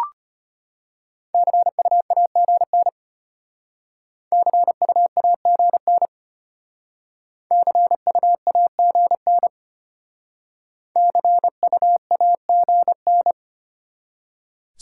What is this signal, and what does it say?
Morse code, heard as a keyed tone, about 33 wpm, 700 Hz. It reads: CUAGN CUAGN CUAGN CUAGN